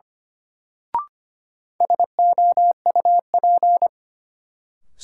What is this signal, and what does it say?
Morse code, heard as a keyed tone, about 25 wpm, 700 Hz. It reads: SOUP
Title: Sentences from Top 1000 Words - Review Part 6 - New Word Drill - 25wpm